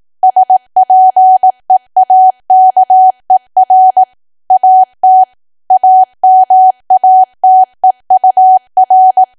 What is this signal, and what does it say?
Morse code, heard as a keyed tone, about 18 wpm, 750 Hz. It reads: SPEAKER AT AMATEUR
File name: Archive / 18 WPM / 18WPM